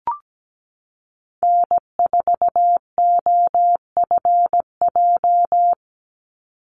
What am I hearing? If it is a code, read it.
Morse code, heard as a keyed tone, about 17 wpm, 700 Hz. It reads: N4OFJ
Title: USA Call Signs 17wpm